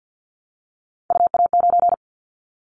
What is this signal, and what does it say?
Morse code, heard as a keyed tone, about 50 wpm, 700 Hz. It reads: UA9
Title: Callsign Prefixes for Top 50 DXCC - 50wpm